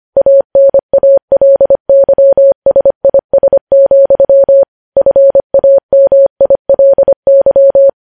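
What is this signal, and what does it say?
Morse code, heard as a keyed tone, about 25 wpm, 550 Hz. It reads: ANALYSIS, FAMILY
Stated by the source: Morse news headlines sent at 25 words per minute.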